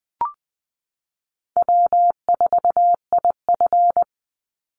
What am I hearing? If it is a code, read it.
Morse code, heard as a keyed tone, about 20 wpm, 700 Hz. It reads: W4IF
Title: Call Signs - 20wpm